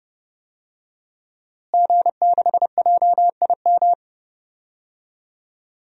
Morse code, heard as a keyed tone, about 32 wpm, 700 Hz. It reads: G6JIM
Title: DXCC-100 Call Signs - 5 Characters - Repeated 2x - 30wpm